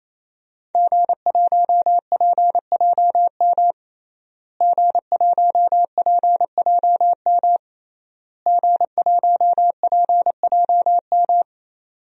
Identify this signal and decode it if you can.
Morse code, heard as a keyed tone, about 28 wpm, 700 Hz. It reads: G1PJM G1PJM G1PJM
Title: DXCC-100 Call Signs - 5 Characters - Repeated 3x - 28wpm